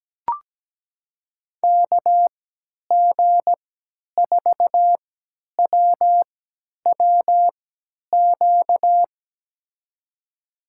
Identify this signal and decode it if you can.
Morse code, heard as a keyed tone, about 17 wpm, spaced as for 10 wpm, 700 Hz. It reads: KG4WWQ
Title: USA Call Signs - 3x Character Spacing - 17wpm